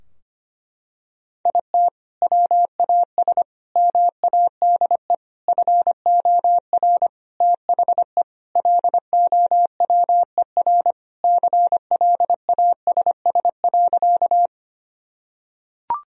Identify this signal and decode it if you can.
Morse code, heard as a keyed tone, about 25 wpm, 700 Hz. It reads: IT WAS MADE FOR THE LOWER CLASS.
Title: Sentences from Top 1000 Words - Review Part 4 - 25wpm